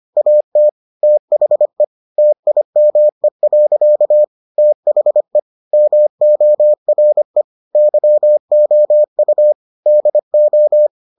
Morse code, heard as a keyed tone, about 25 wpm, 600 Hz. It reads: AT THE TIME. THE MORE YOU DO